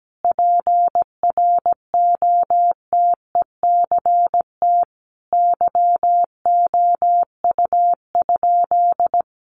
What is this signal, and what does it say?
Morse code, heard as a keyed tone, about 17 wpm, 700 Hz. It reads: PROTECT YOU?
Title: Sentences from Top 900 Words 17wpm